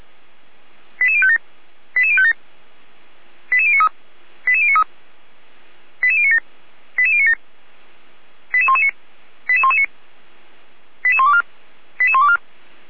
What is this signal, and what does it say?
Five-tone selective call (ZVEI), heard as quick runs of five tones, tones of about 70 ms. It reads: 88057 88057 88093 88093 88097 88097 88119 88119 88124 88124